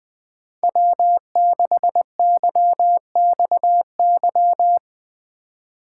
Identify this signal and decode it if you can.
Morse code, heard as a keyed tone, about 21 wpm, 700 Hz. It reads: W6YXY
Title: USA Call Signs - 5 Characters - 3x Character Spacing - 20wpm